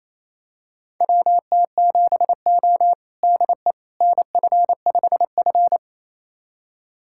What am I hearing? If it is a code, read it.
Morse code, heard as a keyed tone, about 28 wpm, 700 Hz. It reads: WT7O DE NF5F
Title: USA Callsign DE Callsign - 4 Characters - 28wpm